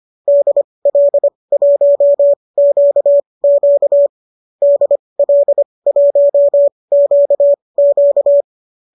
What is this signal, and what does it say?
Morse code, heard as a keyed tone, about 25 wpm, spaced as for 20 wpm, 590 Hz. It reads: DL1QQ DL1QQ
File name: qso219 20